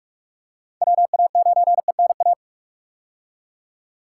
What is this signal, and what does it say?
Morse code, heard as a keyed tone, about 46 wpm, 700 Hz. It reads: WA9ENA
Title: K1USN SST QSO Practice - 45wpm